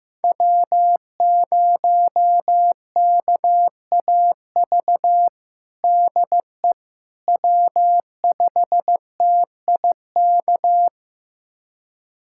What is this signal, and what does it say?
Morse code, heard as a keyed tone, about 15 wpm, 700 Hz. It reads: W0KAV DE W5TIK
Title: USA Callsign DE Callsign - 5 Characters - 15wpm